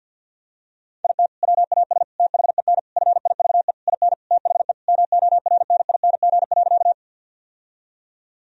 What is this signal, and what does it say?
Morse code, heard as a keyed tone, about 50 wpm, 700 Hz. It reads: IT WAS THEN FIVE IN THE MORNING.
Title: Sentences from Top 1000 Words - Review Part 2 - 50wpm